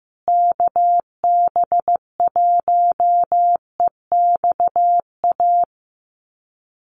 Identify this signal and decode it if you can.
Morse code, heard as a keyed tone, about 15 wpm, 700 Hz. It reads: KB1EXA
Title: USA Call Signs Repeated 3x 15wpm